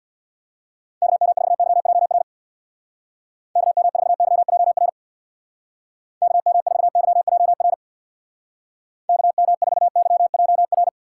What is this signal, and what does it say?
Morse code, heard as a keyed tone, about 62 wpm, 700 Hz. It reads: KM4YJR KM4YJR KM4YJR KM4YJR
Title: Speed Racing - USA 2x3 Call Signs 50wpm